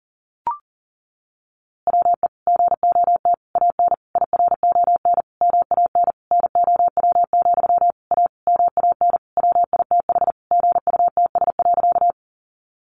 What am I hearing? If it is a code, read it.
Morse code, heard as a keyed tone, about 40 wpm, 700 Hz. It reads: WE GOT AN IRON MAN NOW, A MAN WITH GUTS.
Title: A Sound Of Thunder 40wpm